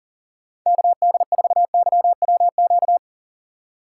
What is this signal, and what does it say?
Morse code, heard as a keyed tone, about 40 wpm, 700 Hz. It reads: KD4YWQ
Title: USA Call Signs - 6 Characters - 4x Character Spacing - 40wpm